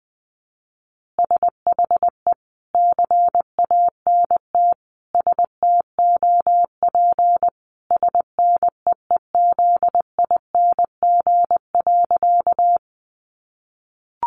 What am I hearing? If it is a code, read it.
Morse code, heard as a keyed tone, about 20 wpm, 700 Hz. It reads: SHE CANT STOP SNEEZING.
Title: Easy Phrases 20wpm